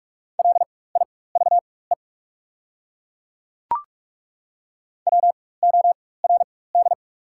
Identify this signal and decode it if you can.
Morse code, heard as a keyed tone, about 48 wpm, spaced as for 21 wpm, 700 Hz. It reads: LIVE WORD
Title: 4x Character Spacing - Top 200 Words - 45wpm